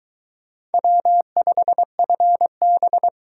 Morse code, heard as a keyed tone, about 22 wpm, 700 Hz. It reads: W5FB